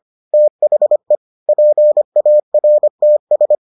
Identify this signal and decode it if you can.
Morse code, heard as a keyed tone, about 25 wpm, 600 Hz. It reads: THE PARTS